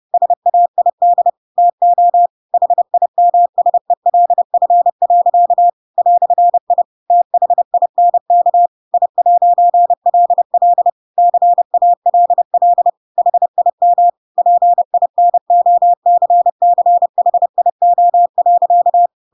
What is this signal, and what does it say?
Morse code, heard as a keyed tone, about 30 wpm, 700 Hz. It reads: SAID TO HIMSELF. "I THINK I'LL CALL HIM PINOCCHIO.
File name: Pinocchio / Pinocchio ch3 30 30 ik1hge